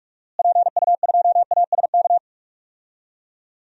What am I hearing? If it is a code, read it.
Morse code, heard as a keyed tone, about 46 wpm, 700 Hz. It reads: PU2ASK
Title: DXCC-100 Call Signs - 6 Characters - Repeated 3x - 45wpm